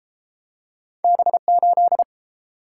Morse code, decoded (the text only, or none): B8